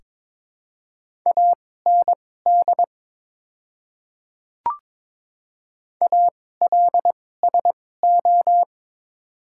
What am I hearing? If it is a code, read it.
Morse code, heard as a keyed tone, about 22 wpm, spaced as for 16 wpm, 700 Hz. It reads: AND ALSO
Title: 2x Character Spacing - Top 200 Words - 22wpm